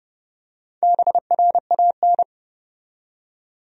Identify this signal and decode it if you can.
Morse code, heard as a keyed tone, about 32 wpm, 700 Hz. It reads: BRAN
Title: USA Common Names - Max 4 Letters - 30wpm